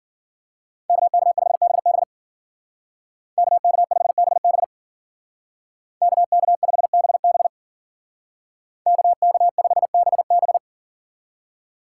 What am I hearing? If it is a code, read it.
Morse code, heard as a keyed tone, about 50 wpm, 700 Hz. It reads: KK5BB KK5BB KK5BB KK5BB
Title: Speed Racing - USA 2x2 Call Signs 40wpm